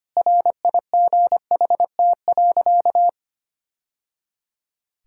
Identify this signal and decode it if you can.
Morse code, heard as a keyed tone, about 25 wpm, 700 Hz. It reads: RIGHT.
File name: practice / 25wpm / Sentences with 3-Word Phrases - 25wpm